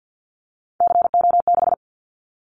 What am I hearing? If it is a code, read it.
Morse code, heard as a keyed tone, about 50 wpm, 700 Hz. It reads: CO6
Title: Callsign Prefixes for Top 100 DXCC - 50wpm